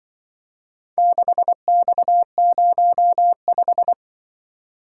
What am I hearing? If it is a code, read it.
Morse code, heard as a keyed tone, about 24 wpm, 700 Hz. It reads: 6X05
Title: Speed-Racing - Four Letter-Number Combinations - 20wpm